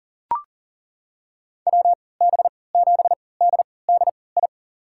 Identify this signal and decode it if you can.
Morse code, heard as a keyed tone, about 40 wpm, spaced as for 23 wpm, 700 Hz. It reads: WB7DDI
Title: USA Call Signs - 6 Characters - 3x Character Spacing - 40wpm